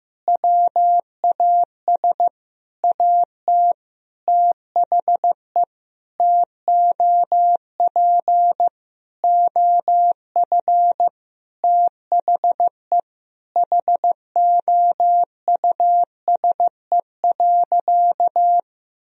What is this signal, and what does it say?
Morse code, heard as a keyed tone, about 15 wpm, 700 Hz. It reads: WAS AT THE TOP OF THE HOUSE.
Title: Sentences from Top 1000 Words - Review Part 2 - 15wpm